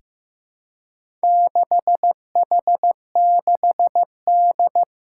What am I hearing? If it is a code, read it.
Morse code, heard as a keyed tone, about 15 wpm, 700 Hz. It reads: 6H6D
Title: Four Challenging Letter-Number Combinations 15wpm